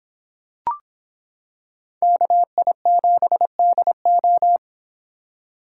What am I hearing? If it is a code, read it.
Morse code, heard as a keyed tone, about 25 wpm, 700 Hz. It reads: KI7DO